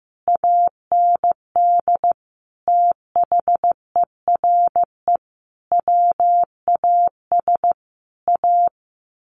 Morse code, heard as a keyed tone, about 15 wpm, 700 Hz. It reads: AND THERE WAS A
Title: A Sound Of Thunder 15wpm